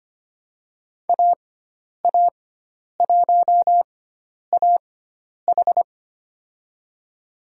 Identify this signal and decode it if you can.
Morse code, heard as a keyed tone, about 25 wpm, spaced as for 10 wpm, 700 Hz. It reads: AA1AH
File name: Lesson 16g - Letter c - Call Signs 25wpm